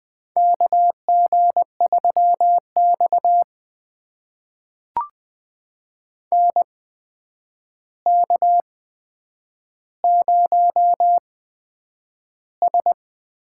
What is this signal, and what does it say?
Morse code, heard as a keyed tone, about 20 wpm, spaced as for 9 wpm, 700 Hz. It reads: KG3X NK0S
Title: USA Call Signs - 4 Characters - 8x Character Spacing - 20wpm